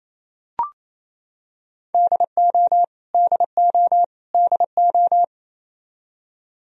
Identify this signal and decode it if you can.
Morse code, heard as a keyed tone, about 28 wpm, 700 Hz. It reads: DO DO DO
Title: Top 100 Words Repeated 28wpm